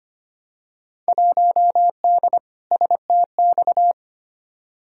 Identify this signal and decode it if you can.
Morse code, heard as a keyed tone, about 25 wpm, 700 Hz. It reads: 1D STX
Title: Field Day Exchanges - 25wpm